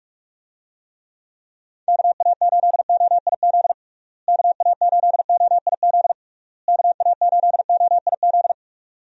Morse code, heard as a keyed tone, about 42 wpm, 700 Hz. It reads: KA8OIZ KA8OIZ KA8OIZ
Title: USA Call Signs - 6 Characters - Repeated 3x - 45wpm